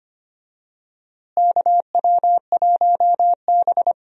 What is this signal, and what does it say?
Morse code, heard as a keyed tone, about 25 wpm, 700 Hz. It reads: KW1B
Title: USA Call Signs - 4 Characters - Repeated 3x - 25wpm